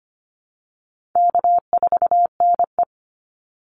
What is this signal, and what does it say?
Morse code, heard as a keyed tone, about 25 wpm, 700 Hz. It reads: K4NE
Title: Easy Call Signs Repeated 2x - 25wpm